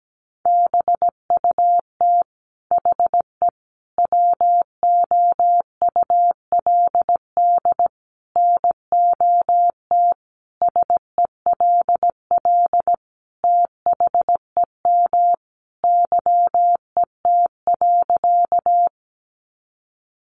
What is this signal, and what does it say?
Morse code, heard as a keyed tone, about 17 wpm, 700 Hz. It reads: BUT HE WOULD NOT SELL THEM YET.
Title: Sentences from Top 1100 Words 17wpm